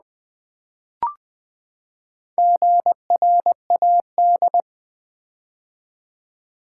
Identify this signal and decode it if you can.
Morse code, heard as a keyed tone, about 20 wpm, 700 Hz. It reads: GRAD
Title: Build A Word - 4 Letters At A Time - 8-Letter Words - 20wpm